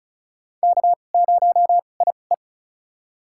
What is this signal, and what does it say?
Morse code, heard as a keyed tone, about 34 wpm, spaced as for 25 wpm, 700 Hz. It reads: K0IE
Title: USA Call Signs - 2x Character Spacing - 35wpm